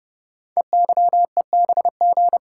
Morse code, heard as a keyed tone, about 30 wpm, 700 Hz. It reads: EYEBG
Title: Code Groups - 5 Letters - 30wpm